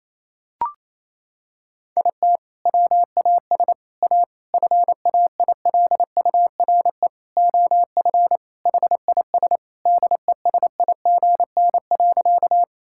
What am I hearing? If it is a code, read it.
Morse code, heard as a keyed tone, about 27 wpm, 700 Hz. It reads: IT WAS A FAILURE OF HIS DESIGN.
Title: Sentences from Top 1000 Words - Review Part 6 - 28wpm